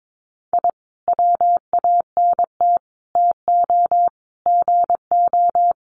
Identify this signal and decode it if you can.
Morse code, heard as a keyed tone, about 22 wpm, 700 Hz. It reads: I WANT TO GO
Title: Sets of 5 Words Encore 22wpm